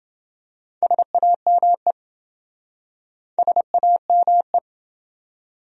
Morse code, heard as a keyed tone, about 28 wpm, 700 Hz. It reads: SAME SAME